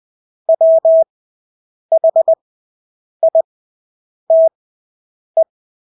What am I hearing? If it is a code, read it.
Morse code, heard as a keyed tone, about 20 wpm, spaced as for 8 wpm, 650 Hz. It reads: WHITE
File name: Engelske ord 20/8 wpm